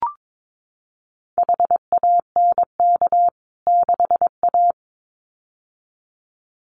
Morse code, heard as a keyed tone, about 22 wpm, 700 Hz. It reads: HANK 6A
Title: CWT Exchange 22wpm